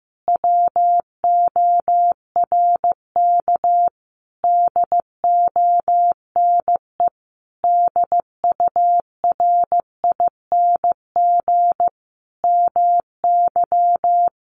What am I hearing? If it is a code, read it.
Morse code, heard as a keyed tone, about 15 wpm, 700 Hz. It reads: WORK DONE DURING MY
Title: Sentences from Top 1300 Words 15wpm